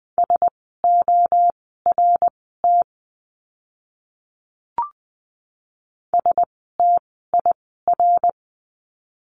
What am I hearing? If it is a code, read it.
Morse code, heard as a keyed tone, about 20 wpm, spaced as for 14 wpm, 700 Hz. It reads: SORT STIR
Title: 054 Lesson 10f - Letter r - Words 20wpm